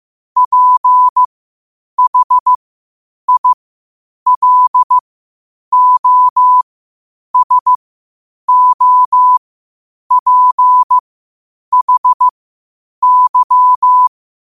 Morse code, heard as a keyed tone, about 15 wpm, spaced as for 8 wpm, 1000 Hz. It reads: PHILOSOPHY
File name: 8wpm